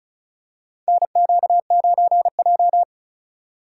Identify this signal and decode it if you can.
Morse code, heard as a keyed tone, about 36 wpm, 700 Hz. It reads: NQ9J
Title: USA Call Signs - 4 Characters - 3x Character Spacing - 35wpm